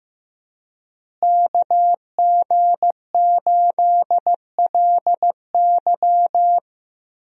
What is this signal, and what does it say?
Morse code, heard as a keyed tone, about 15 wpm, 700 Hz. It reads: KG8LY
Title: USA Call Signs - 5 Characters - Repeated 3x - 15wpm